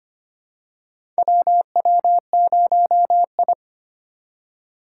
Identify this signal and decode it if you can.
Morse code, heard as a keyed tone, about 25 wpm, 700 Hz. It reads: WW0I
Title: USA Call Signs - 4 Characters - 3x Character Spacing - 25wpm